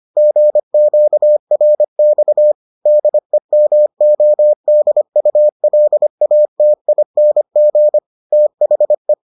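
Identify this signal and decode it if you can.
Morse code, heard as a keyed tone, about 25 wpm, 600 Hz. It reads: GQRX DEMODULATING THE
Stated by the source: Morse edition of the Foundations of Amateur Radio episodes.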